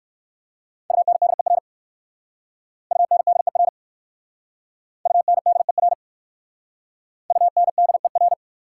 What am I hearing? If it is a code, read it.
Morse code, heard as a keyed tone, about 56 wpm, 700 Hz. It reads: UNDER UNDER UNDER UNDER
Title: Speed-Racing Prefixes and Suffixes 45wpm